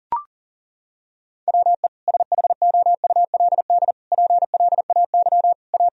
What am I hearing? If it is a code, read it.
Morse code, heard as a keyed tone, about 40 wpm, 700 Hz. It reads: WE SHOULD PLAY A
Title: Sentences from Top 1000 Words - Review Part 3 - 40wpm